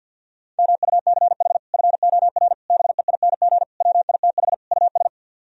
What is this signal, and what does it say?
Morse code, heard as a keyed tone, about 65 wpm, spaced as for 57 wpm, 700 Hz. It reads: MUCH FOR BEING WITH US
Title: Speed-Racing - Sets of 5 Words 50wpm